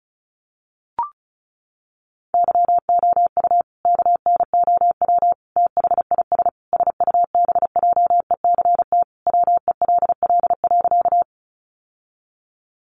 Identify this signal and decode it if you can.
Morse code, heard as a keyed tone, about 34 wpm, 700 Hz. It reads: YOU KNOW THIS SUBJECT WELL.
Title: Sentences from Top 800 Words 35wpm